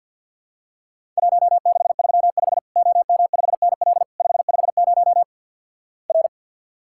Morse code, heard as a keyed tone, about 48 wpm, 700 Hz. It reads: 1635 OM5NL 550 R
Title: Worked All Europe DX Contest - QTC Exchange - 50wpm